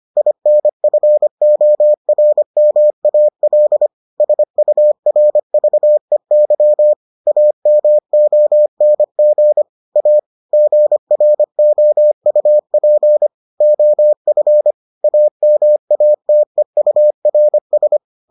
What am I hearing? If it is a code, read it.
Morse code, heard as a keyed tone, about 25 wpm, 600 Hz. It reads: INFORMAL SURVEY AMONG A GROUP OF AMATEURS